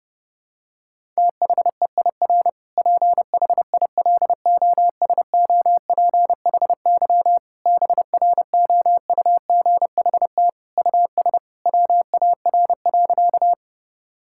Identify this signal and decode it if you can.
Morse code, heard as a keyed tone, about 30 wpm, 700 Hz. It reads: THEIR PHILOSOPHY BROUGHT US WAR.
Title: Sentences from Top 1000 Words - Review Part 6 - 30wpm